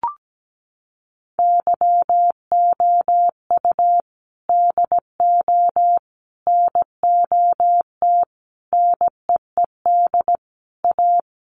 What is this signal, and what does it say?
Morse code, heard as a keyed tone, about 17 wpm, 700 Hz. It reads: YOU DO NOT NEED A
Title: Sets of 5 Words Encore 17wpm